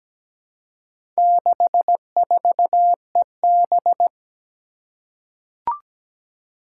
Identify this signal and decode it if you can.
Morse code, heard as a keyed tone, about 17 wpm, 700 Hz. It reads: 64EB
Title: Four Challenging Letter-Number Combinations 17wpm